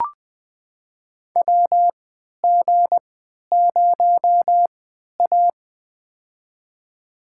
Morse code, heard as a keyed tone, about 20 wpm, spaced as for 11 wpm, 700 Hz. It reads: WG0A
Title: USA Call Signs - 3x Character Spacing - 20wpm